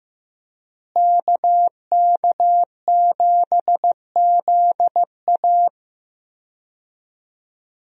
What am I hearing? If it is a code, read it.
Morse code, heard as a keyed tone, about 15 wpm, 700 Hz. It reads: KK7ZA
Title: USA Call Signs - 5 Characters - Repeated 3x - 15wpm